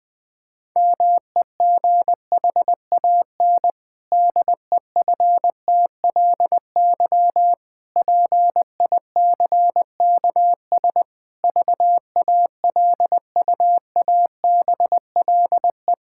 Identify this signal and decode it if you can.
Morse code, heard as a keyed tone, about 20 wpm, 700 Hz. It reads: MEGHAN DEFTLY PICKS VALUABLE
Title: Pangram Challenge v2 - 20wpm